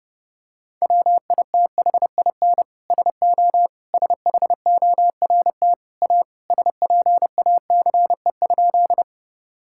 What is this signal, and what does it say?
Morse code, heard as a keyed tone, about 30 wpm, 700 Hz. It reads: WITHIN SO SHORT A SPACE?
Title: Sentences from Top 1000 Words - Review Part 5 - 30wpm